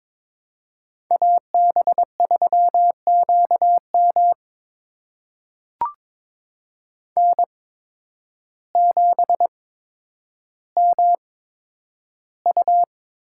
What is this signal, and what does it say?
Morse code, heard as a keyed tone, about 22 wpm, spaced as for 10 wpm, 700 Hz. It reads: AB3QM N7MU
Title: USA Call Signs - 8x Character Spacing - 22wpm